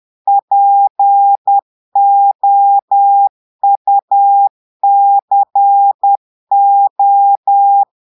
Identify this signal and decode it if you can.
Morse code, heard as a keyed tone, about 10 wpm, 800 Hz. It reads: POUCO
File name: Portugais - texte en clair 1 - vitesse - 10 mots minute